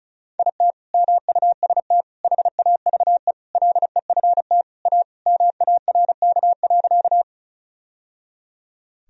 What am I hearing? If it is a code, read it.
Morse code, heard as a keyed tone, about 34 wpm, 700 Hz. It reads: IT MUST HAVE LEFT A MARK.
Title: Sentences from Top 1000 Words - Review Part 1 - 35wpm